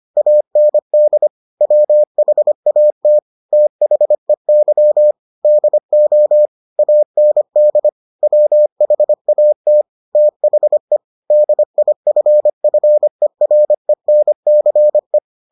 Morse code, heard as a keyed tone, about 25 wpm, 600 Hz. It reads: AND WHAT THEY DO AND WHAT THE DIFFERENCE